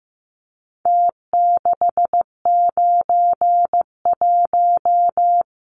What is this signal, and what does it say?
Morse code, heard as a keyed tone, about 15 wpm, 700 Hz. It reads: T691
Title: Numbers - Contest Style - 4-Digit 15wpm